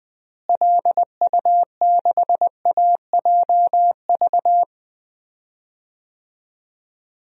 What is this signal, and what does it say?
Morse code, heard as a keyed tone, about 20 wpm, 700 Hz. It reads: LU6AJV
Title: DXCC-100 Call Signs - 6 Characters - Repeated 2x - 20wpm